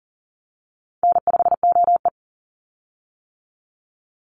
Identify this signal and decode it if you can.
Morse code, heard as a keyed tone, about 40 wpm, 700 Hz. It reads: N5OE
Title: Call Signs - 40wpm